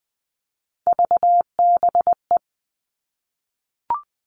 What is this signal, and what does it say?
Morse code, heard as a keyed tone, about 20 wpm, 700 Hz. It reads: VBE
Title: Three Challenging Letter-Number Combinations 20wpm